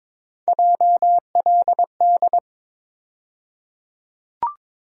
Morse code, heard as a keyed tone, about 22 wpm, 700 Hz. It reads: JLD
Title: USA Call Sign Postfixes - 3 Letters 22wpm